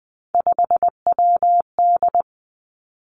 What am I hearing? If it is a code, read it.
Morse code, heard as a keyed tone, about 20 wpm, 700 Hz. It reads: 5WD
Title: Three Challenging Letter-Number Combinations 20wpm